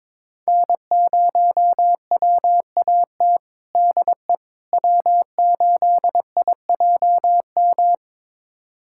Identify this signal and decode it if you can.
Morse code, heard as a keyed tone, about 21 wpm, 700 Hz. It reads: N0WAT DE W8IJM
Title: USA Callsign DE Callsign - 5 Characters - 22wpm